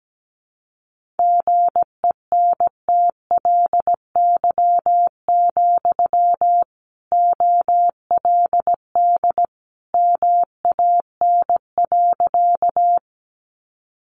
Morse code, heard as a keyed tone, about 17 wpm, 700 Hz. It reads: GENTLY, OLD MAN.
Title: Sentences from Top 1500 Words 17wpm